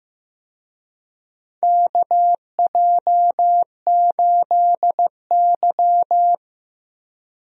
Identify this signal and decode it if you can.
Morse code, heard as a keyed tone, about 15 wpm, 700 Hz. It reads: KJ8Y
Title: USA Call Signs - 4 Characters - Repeated 2x - 15wpm